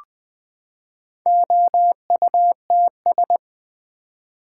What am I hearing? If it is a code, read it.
Morse code, heard as a keyed tone, about 20 wpm, 700 Hz. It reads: OUTS
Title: Build A Word - 4 Letters At A Time - 8-Letter Words - 20wpm